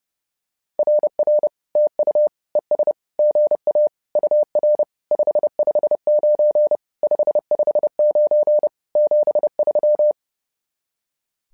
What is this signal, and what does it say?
Morse code, heard as a keyed tone, about 30 wpm, 600 Hz. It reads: RR TU ES GA UR 559 559 73
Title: SOTA QSO Practice 30wpm